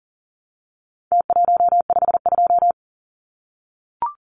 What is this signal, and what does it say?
Morse code, heard as a keyed tone, about 40 wpm, 700 Hz. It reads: T152
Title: Numbers - Contest Style - 4-Digit 40wpm